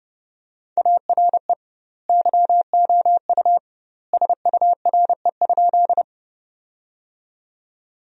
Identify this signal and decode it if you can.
Morse code, heard as a keyed tone, about 30 wpm, 700 Hz. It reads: ARE YOU SURE?
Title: 2x Word-Spacing - Sentences from Top 300 Words - 30wpm